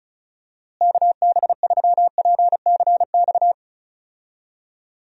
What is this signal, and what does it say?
Morse code, heard as a keyed tone, about 36 wpm, 700 Hz. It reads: KB3PCX
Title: USA Call Signs - 6 Characters - Repeated 3x - 35wpm